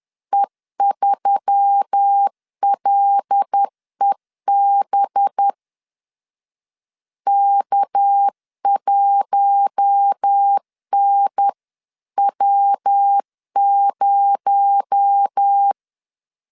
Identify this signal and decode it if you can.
Morse code, heard as a keyed tone, about 10 wpm, 800 Hz. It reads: E3LEB K1NW0
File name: CW / DET / 1 6 1500C S B 10wpm 4137